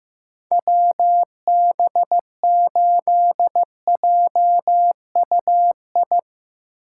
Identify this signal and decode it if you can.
Morse code, heard as a keyed tone, about 15 wpm, 700 Hz. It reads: WB8JUI
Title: K1USN SST QSO Practice - 15wpm